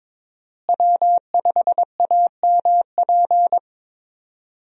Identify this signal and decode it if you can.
Morse code, heard as a keyed tone, about 22 wpm, 700 Hz. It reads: W5AMP
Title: USA Call Signs - 5 Characters - Repeated 2x - 22wpm